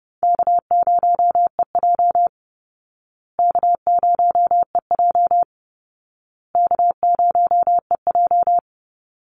Morse code, heard as a keyed tone, about 28 wpm, 700 Hz. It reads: K0EJ K0EJ K0EJ
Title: Call Signs Repeated 2x - 30wpm